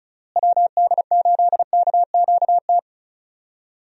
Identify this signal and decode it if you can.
Morse code, heard as a keyed tone, about 34 wpm, 700 Hz. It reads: WD8KQT
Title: USA Call Signs - 6 Characters - Repeated 3x - 35wpm